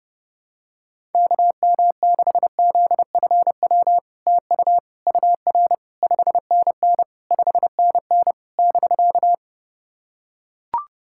Morse code, heard as a keyed tone, about 31 wpm, 700 Hz. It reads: KM6ZFW TU UR 5NN 5NN <BK>
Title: SOTA QSO Practice 30wpm